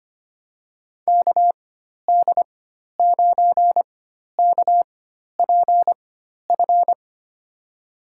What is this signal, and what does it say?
Morse code, heard as a keyed tone, about 25 wpm, spaced as for 12 wpm, 700 Hz. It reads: KD9KPF
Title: USA Call Signs - 6 Characters - 4x Character Spacing - 25wpm